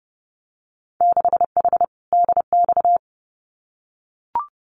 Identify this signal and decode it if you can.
Morse code, heard as a keyed tone, about 30 wpm, 700 Hz. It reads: 6H DX
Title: Winter Field Day Exchanges Encore 30wpm